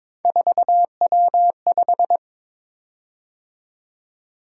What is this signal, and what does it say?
Morse code, heard as a keyed tone, about 22 wpm, 700 Hz. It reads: 4W5
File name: practice / 22wpm / Three Letter-Number Combinations - 22wpm